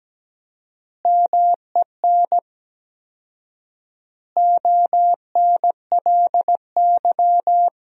Morse code, heard as a keyed tone, about 17 wpm, 700 Hz. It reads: MEN ONLY
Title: 4x Word-Spacing - Sentences from Top 500 Words - 17wpm